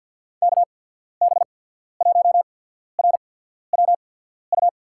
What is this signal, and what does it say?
Morse code, heard as a keyed tone, about 51 wpm, spaced as for 14 wpm, 700 Hz. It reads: KB1RWU
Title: USA Call Signs - 6 Characters - 8x Character Spacing - 50wpm